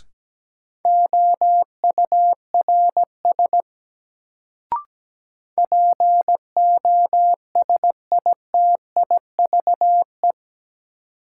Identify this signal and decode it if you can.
Morse code, heard as a keyed tone, about 17 wpm, 700 Hz. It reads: OURS POSITIVE
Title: Sentences from Top 1000 Words - Review Part 5 - New Word Drill - 17wpm